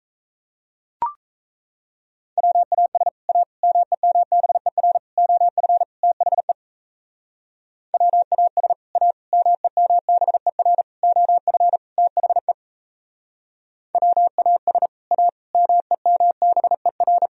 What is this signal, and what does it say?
Morse code, heard as a keyed tone, about 37 wpm, 700 Hz. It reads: WAS A MEMBER OF THE WAS A MEMBER OF THE WAS A MEMBER